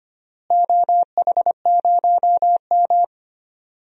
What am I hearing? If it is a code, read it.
Morse code, heard as a keyed tone, about 25 wpm, 700 Hz. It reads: OH0M
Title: DXCC-100 Call Signs - 4 Characters - Repeated 3x - 25wpm